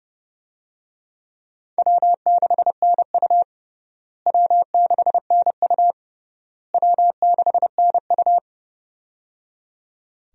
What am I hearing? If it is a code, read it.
Morse code, heard as a keyed tone, about 30 wpm, 700 Hz. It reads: W6NU W6NU W6NU
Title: USA Call Signs - 4 Characters - Repeated 3x - 30wpm